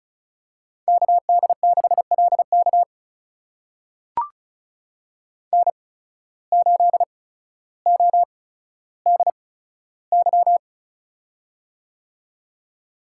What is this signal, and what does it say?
Morse code, heard as a keyed tone, about 36 wpm, spaced as for 15 wpm, 700 Hz. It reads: KD6LK N8ODY
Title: USA Call Signs - 5 Characters - 8x Character Spacing - 35wpm